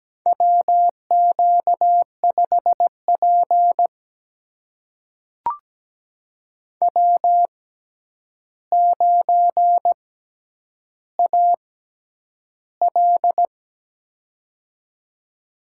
Morse code, heard as a keyed tone, about 17 wpm, spaced as for 9 wpm, 700 Hz. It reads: WQ5P W9AL
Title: USA Call Signs - 4 Characters - 6x Character Spacing - 17wpm